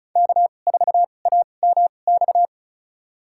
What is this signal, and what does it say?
Morse code, heard as a keyed tone, about 34 wpm, spaced as for 25 wpm, 700 Hz. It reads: K4AMX
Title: USA Call Signs - 5 Characters - 2x Character Spacing - 35wpm